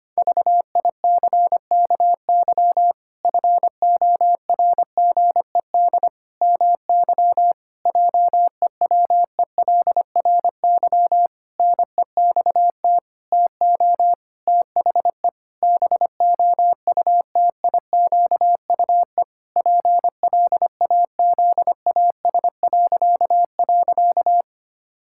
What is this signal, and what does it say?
Morse code, heard as a keyed tone, about 25 wpm, 700 Hz. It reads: VICKY FORGED MY JEWELRY NEXT TO THE BOUTIQUE PLAZAS..
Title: Pangram Challenge v2 - 25wpm